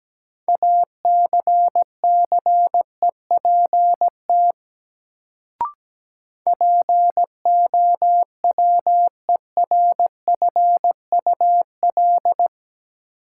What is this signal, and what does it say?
Morse code, heard as a keyed tone, about 17 wpm, 700 Hz. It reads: ACCEPT POWERFUL
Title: Sentences from Top 1300 Words - New Word Drill Alt - 17wpm